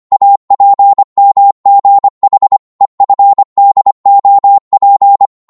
Morse code, heard as a keyed tone, about 25 wpm, 800 Hz. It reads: APMGH EFDOP